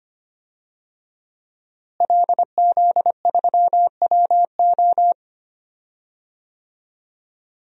Morse code, heard as a keyed tone, about 25 wpm, 700 Hz. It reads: LZ3WO
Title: DXCC-100 Call Signs - 5 Characters - Repeated 2x - 25wpm